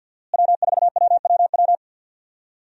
Morse code, heard as a keyed tone, about 46 wpm, 700 Hz. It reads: W4WWW
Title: USA Call Signs - 5 Characters - Repeated 2x - 50wpm